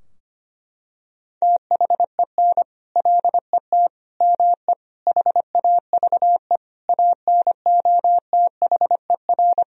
Morse code, heard as a keyed tone, about 25 wpm, 700 Hz. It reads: THEN LET ME HAVE ANOTHER
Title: Sentences from Top 1000 Words - Review Part 2 - 25wpm